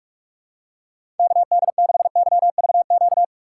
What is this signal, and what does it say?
Morse code, heard as a keyed tone, about 41 wpm, 700 Hz. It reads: KD6YVQ
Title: USA Call Signs - 6 Characters - 4x Character Spacing - 45wpm